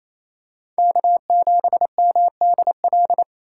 Morse code, heard as a keyed tone, about 28 wpm, 700 Hz. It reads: K7MDL
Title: USA Call Signs - 5 Characters - 8x Character Spacing - 28wpm